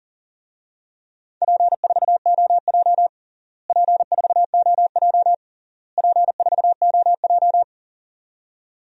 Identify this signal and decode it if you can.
Morse code, heard as a keyed tone, about 40 wpm, 700 Hz. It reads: P4OJ P4OJ P4OJ
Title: DXCC-100 Call Signs - 4 Characters - Repeated 3x - 40wpm